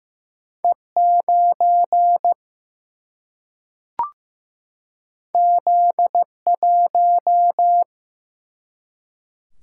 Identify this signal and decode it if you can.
Morse code, heard as a keyed tone, about 15 wpm, 700 Hz. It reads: E9 Z1
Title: Single-Letter Single-Number 15wpm